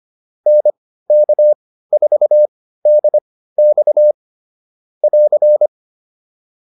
Morse code, heard as a keyed tone, about 25 wpm, spaced as for 15 wpm, 600 Hz. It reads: NK4DX <AR>
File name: QSO110 15